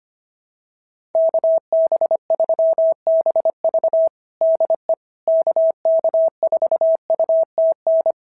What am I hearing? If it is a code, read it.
Morse code, heard as a keyed tone, about 25 wpm, 650 Hz. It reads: KB3BV DE KK4UTN
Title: SOTA QSO Practice 25wpm